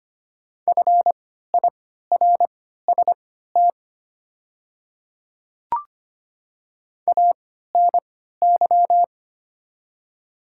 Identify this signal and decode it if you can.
Morse code, heard as a keyed tone, about 25 wpm, spaced as for 14 wpm, 690 Hz. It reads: FIRST ANY
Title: 3x Character Spacing - Top 200 Words - 25wpm